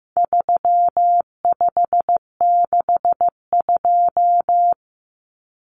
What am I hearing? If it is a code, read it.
Morse code, heard as a keyed tone, about 15 wpm, 700 Hz. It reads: 3562
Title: Numbers - 4-Digit 15wpm